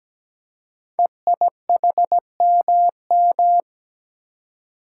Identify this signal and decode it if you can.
Morse code, heard as a keyed tone, about 17 wpm, 700 Hz. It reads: EIHMM
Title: Code Groups - 5 Letters - 17wpm